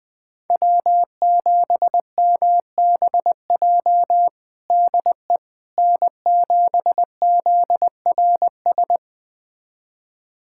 Morse code, heard as a keyed tone, about 21 wpm, 700 Hz. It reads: W7MBJ DE N7ZRS
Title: USA Callsign DE Callsign - 5 Characters - 20wpm